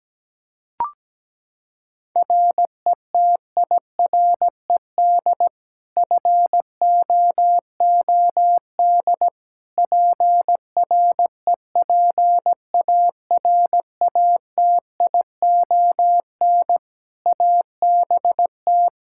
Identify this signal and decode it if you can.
Morse code, heard as a keyed tone, about 17 wpm, 700 Hz. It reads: RETIRED FOOD PREPARATION ABT
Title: K1USN SST QSO Practice 17wpm